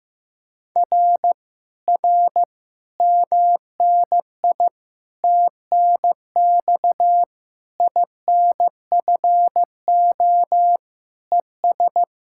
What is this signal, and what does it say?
Morse code, heard as a keyed tone, about 15 wpm, 700 Hz. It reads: R R MNI TNX INFO ES
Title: Speed-Racing Common QSO Phrases Encore - 15wpm